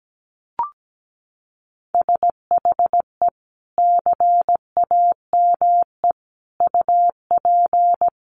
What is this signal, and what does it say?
Morse code, heard as a keyed tone, about 17 wpm, 700 Hz. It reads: SHE CAME UP
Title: Sentences from Top 200 Words 17wpm